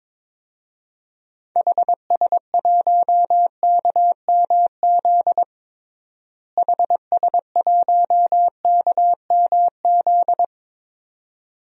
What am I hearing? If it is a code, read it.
Morse code, heard as a keyed tone, about 22 wpm, 700 Hz. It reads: HS1KMZ HS1KMZ
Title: DXCC-100 Call Signs - 6 Characters - Repeated 2x - 22wpm